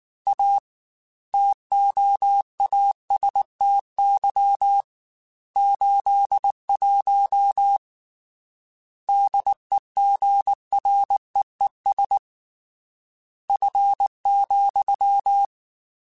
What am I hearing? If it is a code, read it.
Morse code, heard as a keyed tone, about 19 wpm, 800 Hz. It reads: A TOASTY 81 DEGREES F,
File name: Weather Sets / Weather Sets-mp3s / Weather 5 12 @18